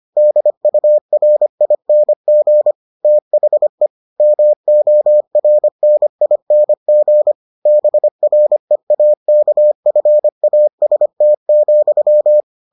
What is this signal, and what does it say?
Morse code, heard as a keyed tone, about 25 wpm, 600 Hz. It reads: DURING THE MORNING BREAKFAST,